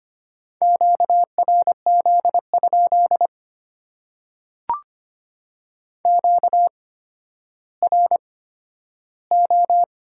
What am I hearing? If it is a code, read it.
Morse code, heard as a keyed tone, about 25 wpm, spaced as for 12 wpm, 700 Hz. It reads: QRZ? QRO
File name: Top 90 QSO Elements - 8x Character Spacing - 25wpm